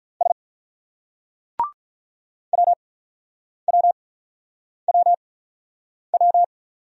Morse code, heard as a keyed tone, about 43 wpm, spaced as for 9 wpm, 700 Hz. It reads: S WWWW
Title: Speed-Racing Single Letters 35wpm